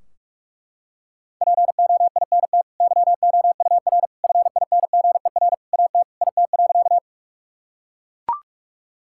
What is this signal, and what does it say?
Morse code, heard as a keyed tone, about 41 wpm, 700 Hz. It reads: POINT YOUR FINGER AT IT.
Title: Sentences from Top 1600 Words 45wpm